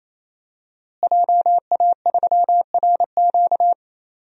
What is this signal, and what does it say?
Morse code, heard as a keyed tone, about 28 wpm, 700 Hz. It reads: JA3RQ
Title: DXCC-100 Call Signs - 5 Characters - Repeated 3x - 28wpm